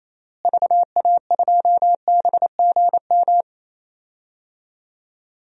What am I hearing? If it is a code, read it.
Morse code, heard as a keyed tone, about 28 wpm, 700 Hz. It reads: VA2BGM
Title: DXCC-100 Call Signs - 6 Characters - Repeated 2x - 28wpm